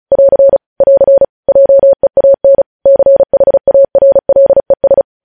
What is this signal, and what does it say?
Morse code, heard as a keyed tone, about 36 wpm, 550 Hz. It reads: <AR> <AR> JEAN CHARLES
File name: podcast / News headlines-1x-35x35